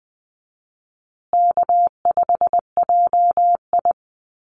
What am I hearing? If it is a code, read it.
Morse code, heard as a keyed tone, about 20 wpm, 700 Hz. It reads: K5JI
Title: USA Call Signs Repeated 3x 20wpm